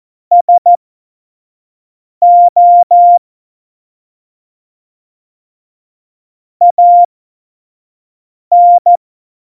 Morse code, heard as a keyed tone, about 14 wpm, spaced as for 5 wpm, 700 Hz. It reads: SO AN
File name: Morsekurs / Witze / Witz 13. Februar 2018, 25 F70